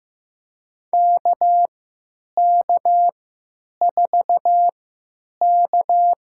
Morse code, heard as a keyed tone, about 15 wpm, spaced as for 9 wpm, 700 Hz. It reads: KK4K
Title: USA Call Signs - 3x Character Spacing - 15wpm